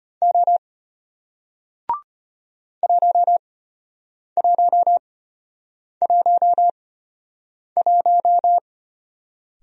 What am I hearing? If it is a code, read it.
Morse code, heard as a keyed tone, about 33 wpm, spaced as for 8 wpm, 700 Hz. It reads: O 1111